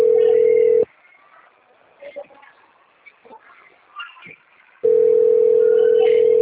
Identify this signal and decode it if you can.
Ringback tone (North America), 440 + 480 Hz, with speech.
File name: rg-1600-unknown- - - .33459